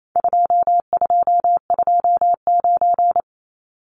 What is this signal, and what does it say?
Morse code, heard as a keyed tone, about 28 wpm, 700 Hz. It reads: 2229
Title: Numbers - 4-Digit 28wpm